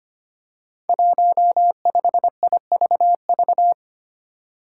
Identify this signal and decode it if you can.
Morse code, heard as a keyed tone, about 25 wpm, 700 Hz. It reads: 15IVV
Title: Five Challenging Letter-Number Combinations 25wpm